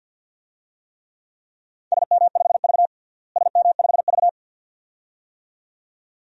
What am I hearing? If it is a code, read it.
Morse code, heard as a keyed tone, about 47 wpm, 700 Hz. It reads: SM5V SM5V
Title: DXCC-100 Call Signs - 4 Characters - Repeated 2x - 50wpm